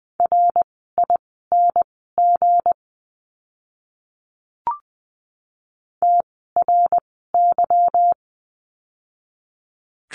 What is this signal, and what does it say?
Morse code, heard as a keyed tone, about 20 wpm, spaced as for 14 wpm, 700 Hz. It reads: RING TRY
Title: 160 Lesson 26f - Letter g - Words 20wpm